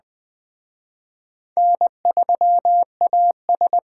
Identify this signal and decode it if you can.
Morse code, heard as a keyed tone, about 21 wpm, 700 Hz. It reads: N3AS
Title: Speed Racing - USA 1x2 Call Signs 17wpm